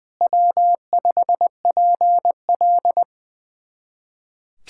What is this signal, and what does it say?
Morse code, heard as a keyed tone, about 21 wpm, 700 Hz. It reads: W5PL
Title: USA Callsign DE Callsign - 4 Characters - 20wpm